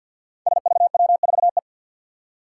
Morse code, heard as a keyed tone, about 50 wpm, 700 Hz. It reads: SVW4E
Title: Five Challenging Letter-Number Combinations 50wpm